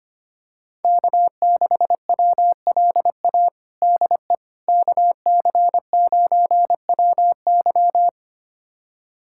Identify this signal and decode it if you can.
Morse code, heard as a keyed tone, about 25 wpm, 700 Hz. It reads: K6WLA DE KC9WY
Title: USA Callsign DE Callsign 25wpm